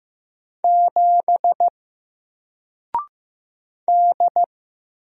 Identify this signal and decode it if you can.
Morse code, heard as a keyed tone, about 15 wpm, 700 Hz. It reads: 7 D
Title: Single Character (Letter:Number:Punctuation:Prosign) 15wpm